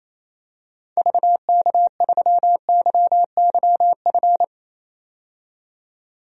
Morse code, heard as a keyed tone, about 27 wpm, 700 Hz. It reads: VK3YYF
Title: DXCC-100 Call Signs - 6 Characters - 28wpm